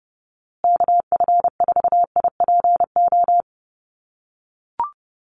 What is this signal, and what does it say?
Morse code, heard as a keyed tone, about 30 wpm, 700 Hz. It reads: KF4IPO
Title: USA Call Signs Repeated 2x 30wpm